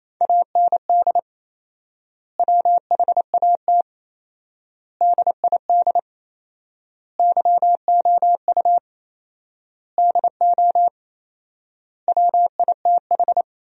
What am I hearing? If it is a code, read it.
Morse code, heard as a keyed tone, about 28 wpm, 700 Hz. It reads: AND WHAT DID YOU DO WITH
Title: 4x Word-Spacing - Sentences from Top 300 Words - 28wpm